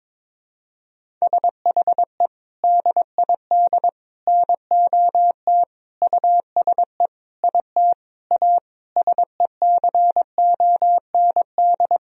Morse code, heard as a keyed tone, about 22 wpm, 700 Hz. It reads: SHE DID NOT USE IT A SECOND
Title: Sentences from Top 1000 Words - Review Part 1 - 22wpm